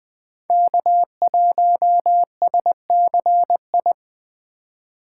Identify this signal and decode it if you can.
Morse code, heard as a keyed tone, about 20 wpm, 700 Hz. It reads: K1SCI